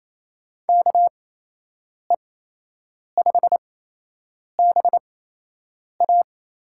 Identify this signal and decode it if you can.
Morse code, heard as a keyed tone, about 28 wpm, spaced as for 8 wpm, 700 Hz. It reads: KE5BA
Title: USA Call Signs - 6 Characters - 8x Character Spacing - 28wpm